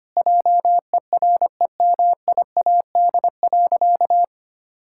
Morse code, heard as a keyed tone, about 25 wpm, 700 Hz. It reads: JEREMIAD.
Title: Pangram Challenge v2 - 25wpm